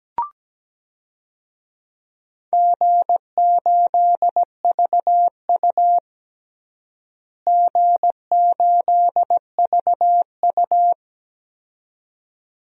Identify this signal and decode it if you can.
Morse code, heard as a keyed tone, about 17 wpm, 700 Hz. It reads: G8VU G8VU
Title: DXCC-100 Call Signs - 4 Characters - Repeated 2x - 17wpm